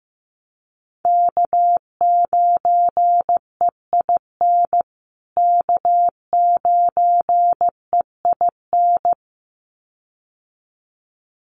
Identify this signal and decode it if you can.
Morse code, heard as a keyed tone, about 15 wpm, 700 Hz. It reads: K9EIN K9EIN
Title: USA Call Signs Repeated 2x 15wpm